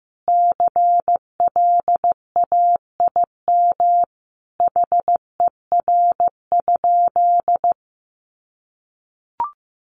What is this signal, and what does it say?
Morse code, heard as a keyed tone, about 15 wpm, 700 Hz. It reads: CLAIM HER?
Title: Sentences from Top 1500 Words 15wpm